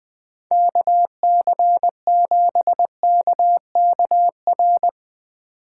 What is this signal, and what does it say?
Morse code, heard as a keyed tone, about 20 wpm, 700 Hz. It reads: KC7KKR
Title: Speed Racing - USA 2x3 Call Signs 17wpm